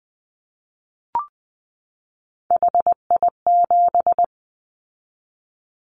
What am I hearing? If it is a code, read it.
Morse code, heard as a keyed tone, about 20 wpm, 700 Hz. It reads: HI7
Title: Callsign Prefixes for Top 100 DXCC - 20wpm